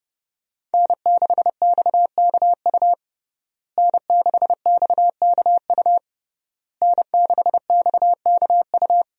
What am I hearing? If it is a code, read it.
Morse code, heard as a keyed tone, about 30 wpm, 700 Hz. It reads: N6XKU N6XKU N6XKU
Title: USA Call Signs - 5 Characters - Repeated 3x - 30wpm